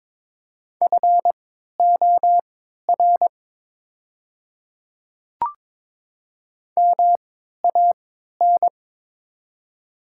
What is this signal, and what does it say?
Morse code, heard as a keyed tone, about 22 wpm, spaced as for 13 wpm, 690 Hz. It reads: FOR MAN
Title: 3x Character Spacing - Top 300 Words - 22wpm